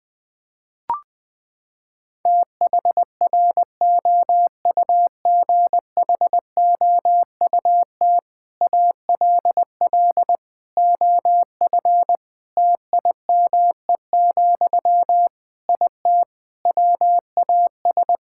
Morse code, heard as a keyed tone, about 20 wpm, 700 Hz. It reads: THROUGHOUT ALL OF TIME, IT WAS
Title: Sentences from Top 1000 Words - Review Part 5 - 20wpm